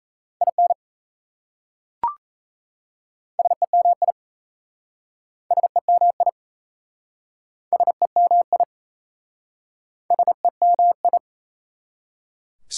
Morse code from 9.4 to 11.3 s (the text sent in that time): SEMI